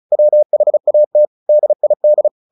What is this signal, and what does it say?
Morse code, heard as a keyed tone, about 34 wpm, 600 Hz. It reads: WHAT DID
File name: the old man and the sea 2 35wpm